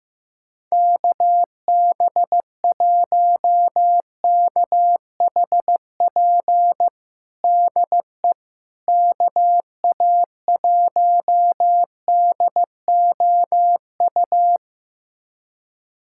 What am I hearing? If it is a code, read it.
Morse code, heard as a keyed tone, about 15 wpm, 700 Hz. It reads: KB1KHP DE KA1DOU
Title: USA Callsign DE Callsign 15wpm